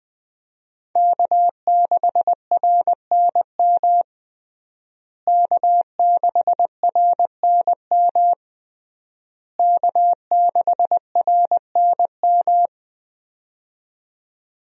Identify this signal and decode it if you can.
Morse code, heard as a keyed tone, about 20 wpm, 700 Hz. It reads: K6RNM K6RNM K6RNM
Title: USA Call Signs - 5 Characters - Repeated 3x - 20wpm